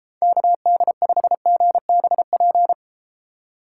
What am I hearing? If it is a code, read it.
Morse code, heard as a keyed tone, about 33 wpm, 700 Hz. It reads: KD5GBP